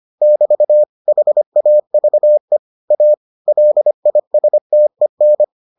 Morse code, heard as a keyed tone, about 25 wpm, 600 Hz. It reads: <BT> HAVE A LISTEN